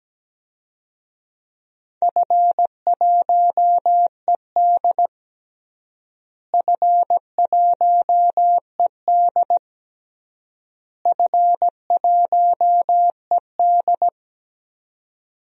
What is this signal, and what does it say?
Morse code, heard as a keyed tone, about 17 wpm, 700 Hz. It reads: F1ED F1ED F1ED
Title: DXCC-100 Call Signs - 4 Characters - Repeated 3x - 17wpm